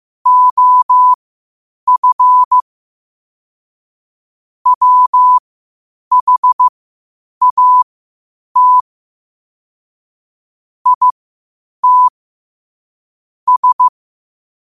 Morse code, heard as a keyed tone, about 15 wpm, spaced as for 8 wpm, 1000 Hz. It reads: OF WHAT IT S
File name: qotdfiles / 8wpm